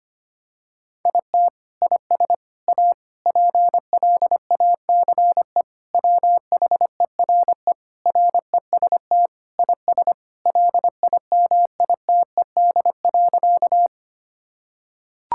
Morse code, heard as a keyed tone, about 25 wpm, 700 Hz. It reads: IT IS A PLACE WHERE REST IS LIMITED.
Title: Sentences from Top 1000 Words - Review Part 6 - 25wpm